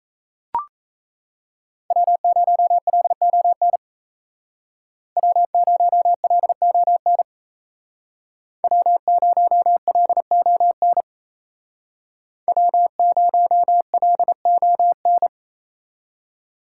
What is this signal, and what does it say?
Morse code, heard as a keyed tone, about 35 wpm, 700 Hz. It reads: W0LON W0LON W0LON W0LON